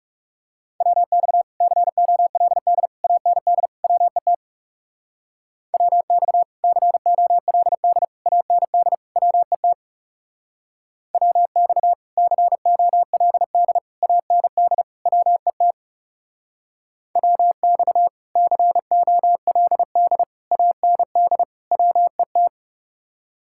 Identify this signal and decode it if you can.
Morse code, heard as a keyed tone, about 36 wpm, 700 Hz. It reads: WX COLD AND WET WX COLD AND WET WX COLD AND WET WX COLD AND WET
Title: Speed-Racing Common QSO Phrases - 30wpm